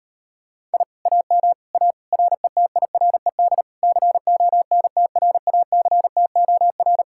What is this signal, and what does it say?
Morse code, heard as a keyed tone, about 39 wpm, 700 Hz. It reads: I AM A RETIRED CONTRACTOR